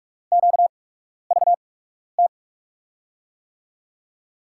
Morse code, heard as a keyed tone, about 44 wpm, spaced as for 12 wpm, 700 Hz. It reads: QVT
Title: USA Call Signs - 6 Characters - 8x Character Spacing - 45wpm